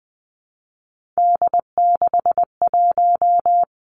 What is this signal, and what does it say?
Morse code, heard as a keyed tone, about 20 wpm, 700 Hz. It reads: D61
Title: Three Challenging Letter-Number Combinations 20wpm